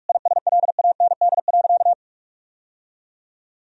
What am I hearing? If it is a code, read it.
Morse code, heard as a keyed tone, about 46 wpm, 700 Hz. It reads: ISLAND.
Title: Sentences from Top 1000 Words - Review Part 3 - 45wpm